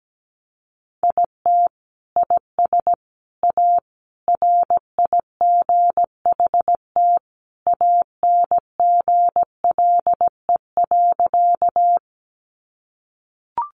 Morse code, heard as a keyed tone, about 17 wpm, 700 Hz. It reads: IT IS A RIGHT ANGLE.
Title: Sentences from Top 1500 Words 17wpm